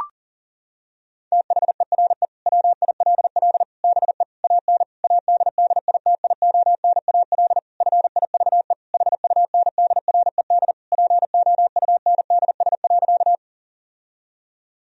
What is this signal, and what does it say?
Morse code, heard as a keyed tone, about 40 wpm, 700 Hz. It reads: THERE WILL BE AN ADDITIONAL FIVE HUNDRED POUNDS.
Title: Sentences from Top 1000 Words - Review Part 4 - 40wpm